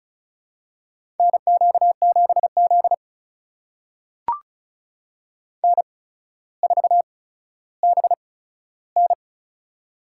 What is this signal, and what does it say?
Morse code, heard as a keyed tone, about 36 wpm, spaced as for 15 wpm, 700 Hz. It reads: NQ7Z N4BN
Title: USA Call Signs - 4 Characters - 8x Character Spacing - 35wpm